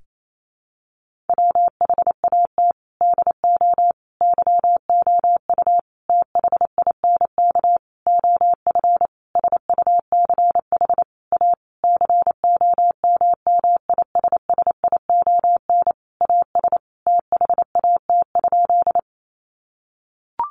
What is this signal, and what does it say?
Morse code, heard as a keyed tone, about 28 wpm, 700 Hz. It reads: WHAT DO YOU THINK OF SUCH A COMMISSION AS THAT?
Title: Sentences from Top 1400 Words 28wpm